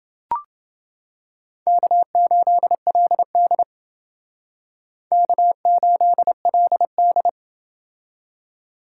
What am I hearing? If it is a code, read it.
Morse code, heard as a keyed tone, about 30 wpm, 700 Hz. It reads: K8LD K8LD